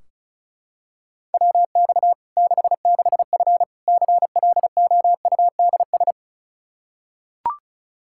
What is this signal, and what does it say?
Morse code, heard as a keyed tone, about 36 wpm, 700 Hz. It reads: WX 66F CLOUDS
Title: Practice QSO - Exchange 2 - Info - 35wpm